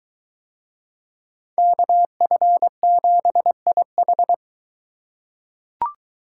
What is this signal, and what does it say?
Morse code, heard as a keyed tone, about 21 wpm, 700 Hz. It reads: KF7IH